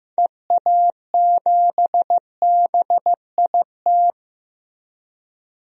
Morse code, heard as a keyed tone, about 15 wpm, 700 Hz. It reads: EA7BIT
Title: DXCC-100 Call Signs - 6 Characters - Repeated 2x - 15wpm